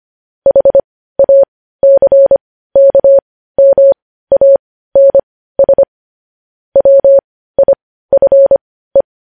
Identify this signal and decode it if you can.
Morse code, heard as a keyed tone, about 25 wpm, spaced as for 15 wpm, 550 Hz. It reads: HACKMANS WIFE
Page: News Headlines in Morse Code at 15 WPM